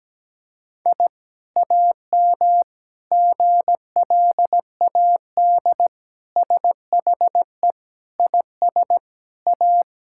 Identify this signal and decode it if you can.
Morse code, heard as a keyed tone, about 17 wpm, 700 Hz. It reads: I AM GLAD SHE IS A
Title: Sentences from Top 1000 Words - Review Part 6 - 17wpm